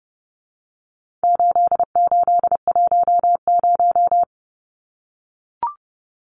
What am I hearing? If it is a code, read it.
Morse code, heard as a keyed tone, about 30 wpm, 700 Hz. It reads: 8810
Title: Numbers - 4-Digit 30wpm